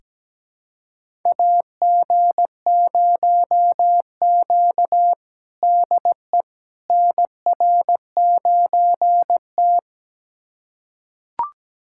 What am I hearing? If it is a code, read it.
Morse code, heard as a keyed tone, about 17 wpm, 700 Hz. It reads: AG0Q DE NR9T
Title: USA Callsign DE Callsign - 4 Characters - 17wpm